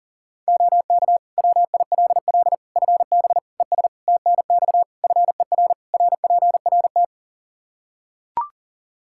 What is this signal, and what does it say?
Morse code, heard as a keyed tone, about 40 wpm, 700 Hz. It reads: OK WILL FB ES TNX FER RPRT
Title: Practice QSO - Exchange 2 - Info - 40wpm